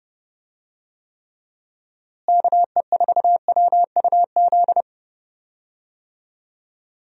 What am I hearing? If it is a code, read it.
Morse code, heard as a keyed tone, about 30 wpm, 700 Hz. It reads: KE4WUZ
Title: USA Call Signs - 6 Characters - Repeated 3x - 30wpm